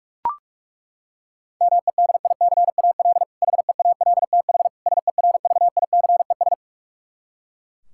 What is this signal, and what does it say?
Morse code, heard as a keyed tone, about 41 wpm, 700 Hz. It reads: MEDICAL HEALTH SERVICES
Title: K1USN SST QSO Practice 45wpm